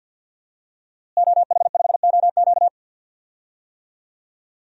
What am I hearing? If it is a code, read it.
Morse code, heard as a keyed tone, about 47 wpm, 700 Hz. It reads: OH5OQ
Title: DXCC-100 Call Signs - 5 Characters - Repeated 3x - 50wpm